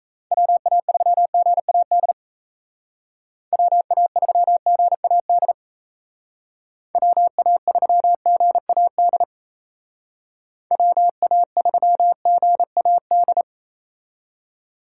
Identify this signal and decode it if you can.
Morse code, heard as a keyed tone, about 36 wpm, 700 Hz. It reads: WA3GAD WA3GAD WA3GAD WA3GAD